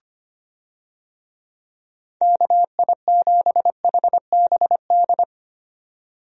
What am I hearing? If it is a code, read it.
Morse code, heard as a keyed tone, about 25 wpm, 700 Hz. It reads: KI7HBD
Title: USA Call Signs - 6 Characters - 25wpm